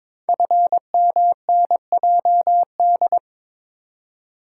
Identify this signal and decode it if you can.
Morse code, heard as a keyed tone, about 23 wpm, 700 Hz. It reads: FMNJD
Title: Code Groups - 5 Letters - 22wpm